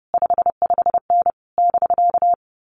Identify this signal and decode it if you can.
Morse code, heard as a keyed tone, about 30 wpm, 700 Hz. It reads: 55N <BK>
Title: Common QSO Phrases Encore 30wpm